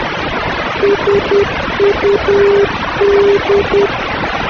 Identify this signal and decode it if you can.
Morse code, heard as a keyed tone, about 10 wpm, 400 Hz, with noise. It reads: SUD
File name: swli05639 / SUD NDB CHANIA